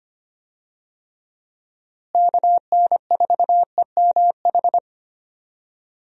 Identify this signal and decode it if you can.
Morse code, heard as a keyed tone, about 25 wpm, 700 Hz. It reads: KN4EMH
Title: USA Call Signs - 6 Characters - Repeated 3x - 25wpm